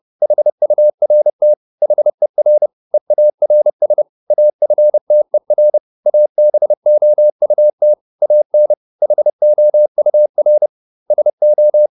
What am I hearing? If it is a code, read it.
Morse code, heard as a keyed tone, about 30 wpm, 600 Hz. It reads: HURT HER EARS AFTER ABOUT AN HOUR SO